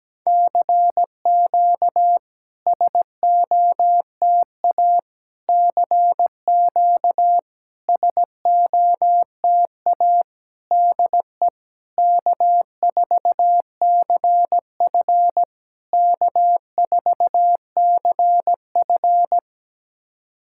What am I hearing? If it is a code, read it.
Morse code, heard as a keyed tone, about 17 wpm, 700 Hz. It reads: CQ SOTA CQ SOTA DE K4CF K4CF
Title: SOTA QSO Practice 17wpm